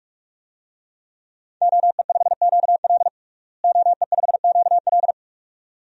Morse code, heard as a keyed tone, about 46 wpm, 700 Hz. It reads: OE5QL OE5QL
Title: DXCC-100 Call Signs - 5 Characters - Repeated 2x - 45wpm